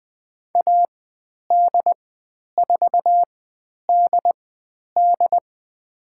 Morse code, heard as a keyed tone, about 20 wpm, spaced as for 10 wpm, 700 Hz. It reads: AD4DD
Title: 095 Lesson 16g - Letter c - Call Signs 20wpm